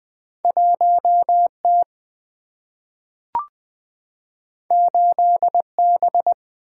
Morse code, heard as a keyed tone, about 20 wpm, 700 Hz. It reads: 1 T 8 B
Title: Single-Letter Single-Number 20wpm